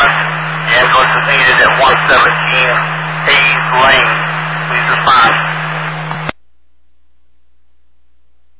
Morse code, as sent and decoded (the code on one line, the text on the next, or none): none